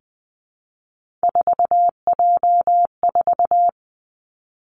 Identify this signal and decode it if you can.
Morse code, heard as a keyed tone, about 20 wpm, 700 Hz. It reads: 4J4
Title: Three Challenging Letter-Number Combinations 20wpm